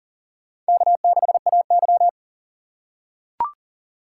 Morse code, heard as a keyed tone, about 40 wpm, 700 Hz. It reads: K6AY
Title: USA Call Signs - 4 Characters - 3x Character Spacing - 40wpm